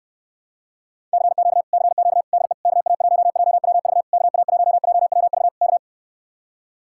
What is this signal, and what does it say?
Morse code, heard as a keyed tone, about 69 wpm, 700 Hz. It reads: CQ CQ DE KA1JWV KA1JWV K